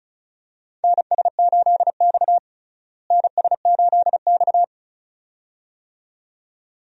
Morse code, heard as a keyed tone, about 36 wpm, 700 Hz. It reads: NS8X NS8X
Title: USA Call Signs - 4 Characters - Repeated 2x - 35wpm